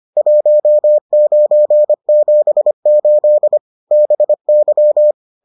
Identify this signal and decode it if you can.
Morse code, heard as a keyed tone, about 25 wpm, 600 Hz. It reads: 1978 BY